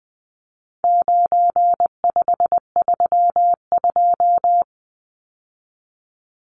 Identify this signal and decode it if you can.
Morse code, heard as a keyed tone, about 20 wpm, 700 Hz. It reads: 9532
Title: Numbers - 4-Digit 20wpm